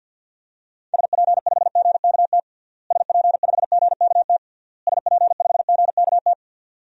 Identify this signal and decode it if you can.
Morse code, heard as a keyed tone, about 51 wpm, 700 Hz. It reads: SP5GKT SP5GKT SP5GKT
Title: DXCC-100 Call Signs - 6 Characters - Repeated 3x - 50wpm